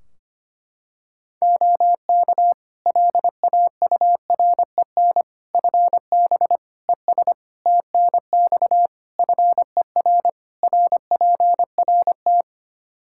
Morse code, heard as a keyed tone, about 25 wpm, 700 Hz. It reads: OK LAUREN FB ES TNX FER RPRT
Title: Practice QSO - Exchange 2 - Info - 25wpm